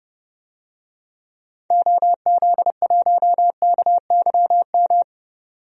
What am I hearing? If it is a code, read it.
Morse code, heard as a keyed tone, about 32 wpm, 700 Hz. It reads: OZ1KYM
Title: DXCC-100 Call Signs - 6 Characters - 30wpm